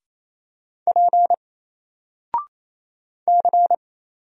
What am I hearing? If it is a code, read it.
Morse code, heard as a keyed tone, about 27 wpm, 700 Hz. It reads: P C
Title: Single Letter-Number Alt 28wpm